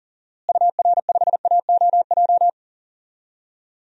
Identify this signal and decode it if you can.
Morse code, heard as a keyed tone, about 40 wpm, 700 Hz. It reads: UR5AOJ
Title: DXCC-100 Call Signs - 6 Characters - Repeated 2x - 40wpm